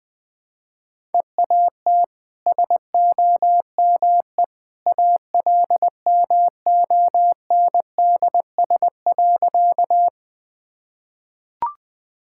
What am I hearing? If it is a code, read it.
Morse code, heard as a keyed tone, about 20 wpm, 700 Hz. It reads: EAT SOME ALMONDS.
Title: Easy Sentences with Hints - 20wpm